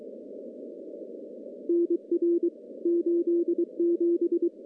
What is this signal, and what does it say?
Morse code, heard as a keyed tone, about 23 wpm, 340 Hz. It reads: NR 87
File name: contests / CW-SS-k5tr- -1500